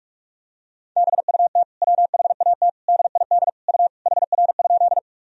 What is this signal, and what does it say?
Morse code, heard as a keyed tone, about 46 wpm, 700 Hz. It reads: BUT WHAT DID U HR?
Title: Sentences from Top 300 Words and Abbreviations 45wpm